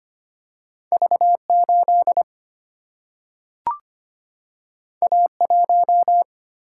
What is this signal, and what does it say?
Morse code, heard as a keyed tone, about 25 wpm, 700 Hz. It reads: V 8 A 1
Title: Single-Letter Single-Number 25wpm